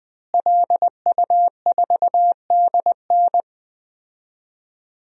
Morse code, heard as a keyed tone, about 21 wpm, 700 Hz. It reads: LU4DN
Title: DXCC-100 Call Signs - 5 Characters - Repeated 3x - 20wpm